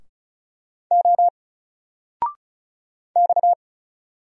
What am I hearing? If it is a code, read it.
Morse code, heard as a keyed tone, about 36 wpm, 700 Hz. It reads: O X
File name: Single-Letter-Number-35wpm